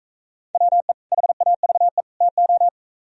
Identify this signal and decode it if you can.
Morse code, heard as a keyed tone, about 42 wpm, 700 Hz. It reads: WE HAVE TO